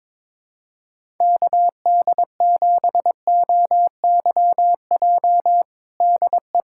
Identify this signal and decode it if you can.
Morse code, heard as a keyed tone, about 22 wpm, 700 Hz. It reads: KD7OYJ DE
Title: USA Callsign DE Callsign - 6 Characters - 22wpm